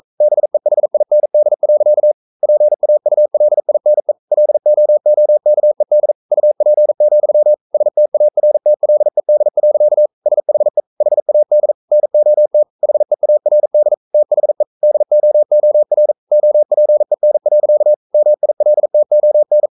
Morse code, heard as a keyed tone, about 42 wpm, 600 Hz. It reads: BEHIND. PAULINE LOOKED UP, STARTLED. SHE HAD NOT HEARD THE DOOR OPEN. MILTON
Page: Romanen Never af Ken Follet på engelsk.